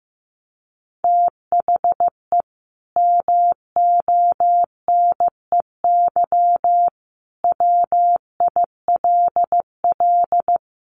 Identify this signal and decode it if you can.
Morse code, heard as a keyed tone, about 15 wpm, 700 Hz. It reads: THE MONEY WILL
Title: Sentences from Top 1500 Words 15wpm